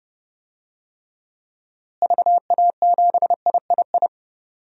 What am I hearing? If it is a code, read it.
Morse code, heard as a keyed tone, about 30 wpm, 700 Hz. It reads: VA7III
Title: DXCC-100 Call Signs - 6 Characters - Repeated 3x - 30wpm